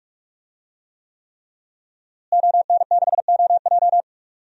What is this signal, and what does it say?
Morse code, heard as a keyed tone, about 46 wpm, 700 Hz. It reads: ON6OJ
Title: DXCC-100 Call Signs - 5 Characters - Repeated 3x - 45wpm